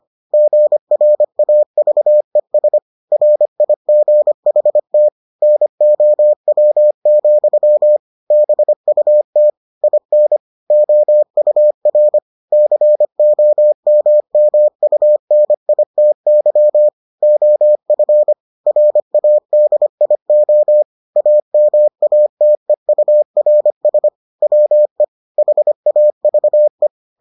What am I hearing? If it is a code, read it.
Morse code, heard as a keyed tone, about 25 wpm, 600 Hz. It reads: GRAVES RIGHT NOW, BUT IN OUR COMMUNITY OF RADIO AMATEURS WE HAVE